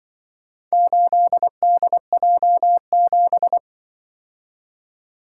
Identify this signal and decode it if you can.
Morse code, heard as a keyed tone, about 24 wpm, 700 Hz. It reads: 8DJ7
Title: Speed-Racing - Four Letter-Number Combinations - 20wpm